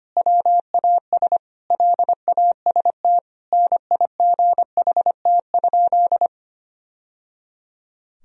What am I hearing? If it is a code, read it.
Morse code, heard as a keyed tone, about 25 wpm, 700 Hz. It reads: WAS LAST NIGHT?
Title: Sentences from Top 1000 Words - Review Part 1 - 25wpm